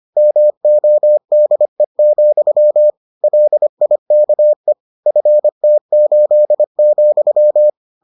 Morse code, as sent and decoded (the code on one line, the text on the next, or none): -- --- -.. --..-- / .-.. .. -.- . / ..-. - ---.. --..--
MODE, LIKE FT8,